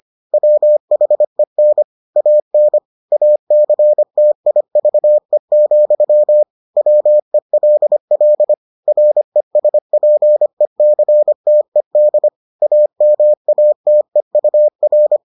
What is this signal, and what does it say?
Morse code, heard as a keyed tone, about 25 wpm, 600 Hz. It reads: WHEN AN ACTIVE, WELL RESPECTED AMATEUR